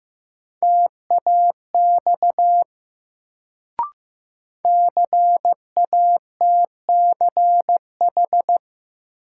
Morse code, heard as a keyed tone, about 15 wpm, 700 Hz. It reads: TAX CATCH
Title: Sentences from Top 1500 Words - New Word Drill Alt - 15wpm